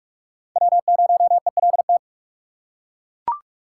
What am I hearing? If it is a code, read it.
Morse code, heard as a keyed tone, about 42 wpm, 700 Hz. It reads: W0ELT
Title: USA Call Signs - 4x Character Spacing - 45wpm